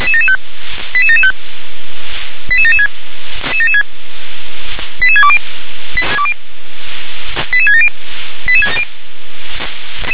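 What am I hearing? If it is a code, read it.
Five-tone selective call (ZVEI), heard as quick runs of five tones, tones of about 70 ms. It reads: 88775 88776 88620 88670
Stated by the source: Received on SDR as ZVEI five-tone selective call.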